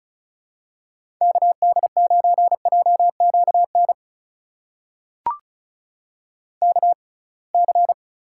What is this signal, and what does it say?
Morse code, heard as a keyed tone, about 34 wpm, spaced as for 26 wpm, 700 Hz. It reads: KD9JQN KC
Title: USA Call Signs - 6 Characters - 6x Character Spacing - 35wpm